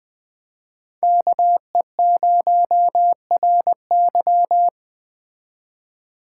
Morse code, heard as a keyed tone, about 20 wpm, 700 Hz. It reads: KE0RY
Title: Speed Racing - USA Call Signs 17wpm